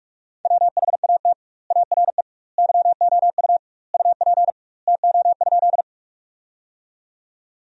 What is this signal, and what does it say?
Morse code, heard as a keyed tone, about 46 wpm, 700 Hz. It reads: WHAT ARE YOU UP TO?
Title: 2x Word-Spacing - Sentences from Top 100 Words - 45wpm